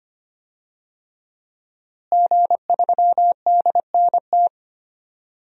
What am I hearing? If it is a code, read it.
Morse code, heard as a keyed tone, about 25 wpm, 700 Hz. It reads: G3DNT
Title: DXCC-100 Call Signs - 5 Characters - Repeated 3x - 25wpm